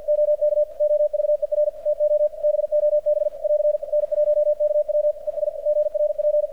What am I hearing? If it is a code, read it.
Morse code, heard as a keyed tone, about 46 wpm, 600 Hz, with radio noise. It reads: OK OVER TO BOB <AR> N1KW ES GRP
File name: archived / CRSnet